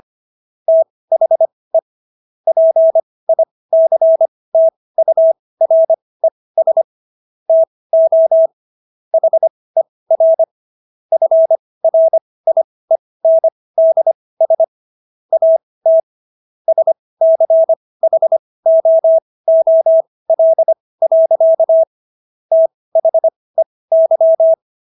Morse code, heard as a keyed tone, about 25 wpm, spaced as for 18 wpm, 650 Hz. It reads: THE PICTURES TO HER FRIENDS AT SCHOOL. THEY